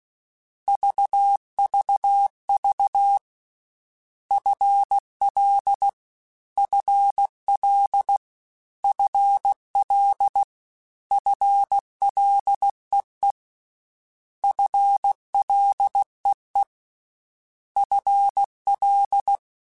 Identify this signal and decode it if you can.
Morse code, heard as a keyed tone, about 16 wpm, 800 Hz. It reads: VVV FL FL FL FLEE FLEE FL
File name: Combinations / Combinations-mp3s / Combinations@15